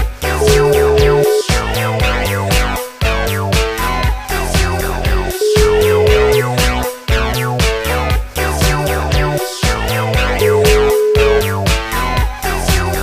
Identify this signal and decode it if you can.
Ringback tone (Europe, 425 Hz), with noise.